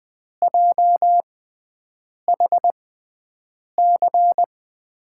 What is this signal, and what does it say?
Morse code, heard as a keyed tone, about 20 wpm, spaced as for 7 wpm, 700 Hz. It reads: JHC
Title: USA Call Signs - 6x Character Spacing - 20wpm